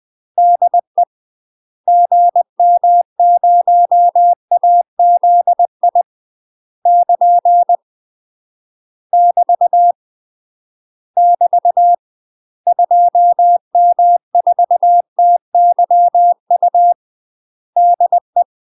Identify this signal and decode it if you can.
Morse code, heard as a keyed tone, about 20 wpm, 700 Hz. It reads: DE GM0AZI <KN> <BT> <BT> 2M4TYU DE
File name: UK MASTER QSOs TRAINING / 20WPM / UK MASTER QSOs 0004